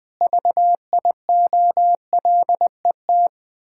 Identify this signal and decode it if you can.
Morse code, heard as a keyed tone, about 20 wpm, 700 Hz. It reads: VIOLET
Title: Pangram Challenge v2 - 20wpm